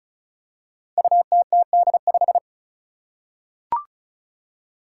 Morse code, heard as a keyed tone, about 36 wpm, 700 Hz. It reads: UTTD5
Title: Five Challenging Letter-Number Combinations 35wpm